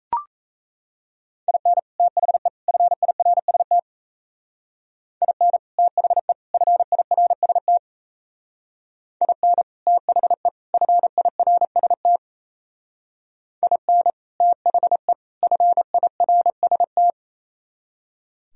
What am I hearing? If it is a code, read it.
Morse code, heard as a keyed tone, about 34 wpm, 700 Hz. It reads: IN THE FIRST IN THE FIRST IN THE FIRST IN THE FIRST
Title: Speed-Racing - Sets of 3 Words 28wpm